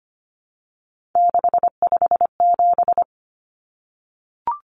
Morse code, heard as a keyed tone, about 25 wpm, 690 Hz. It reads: 657
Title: Numbers - 3-Digit 25wpm